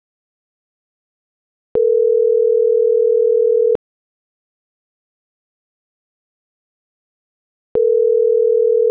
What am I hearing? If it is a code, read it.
Ringback tone (North America), 440 + 480 Hz.